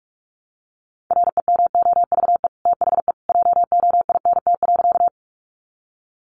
Morse code, heard as a keyed tone, about 44 wpm, 700 Hz. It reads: REMOVE THE JOINT.
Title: Sentences from Top 1300 Words 45wpm